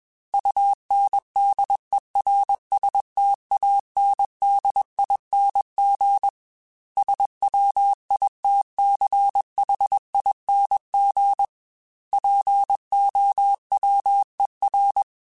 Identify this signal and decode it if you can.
Morse code, heard as a keyed tone, about 21 wpm, 800 Hz. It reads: UNDERSTANDING SWITCHING POWER
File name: Technical Subjects / Technical Subjects-mp3s / Sw Pwr Sup @20